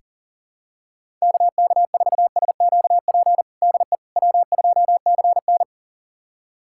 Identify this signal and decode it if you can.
Morse code, heard as a keyed tone, about 40 wpm, 700 Hz. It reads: KK4SQP DE W2CN
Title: USA Callsign DE Callsign 40wpm